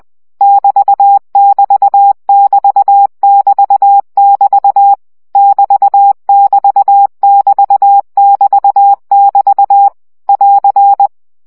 Morse code, heard as a keyed tone, about 20 wpm, 800 Hz. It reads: <BT><BT><BT><BT><BT> <BT><BT><BT><BT><BT> <AR>
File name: CW gyakorlas egyenlo 17 baud